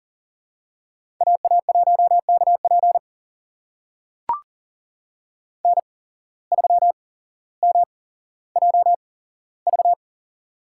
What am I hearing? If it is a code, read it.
Morse code, heard as a keyed tone, about 38 wpm, spaced as for 17 wpm, 700 Hz. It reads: AA1KP N3MJV
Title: USA Call Signs - 5 Characters - 8x Character Spacing - 40wpm